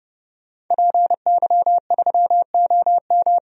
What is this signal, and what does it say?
Morse code, heard as a keyed tone, about 29 wpm, 700 Hz. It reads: PY3OM
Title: DXCC-100 Call Signs - 5 Characters - 30wpm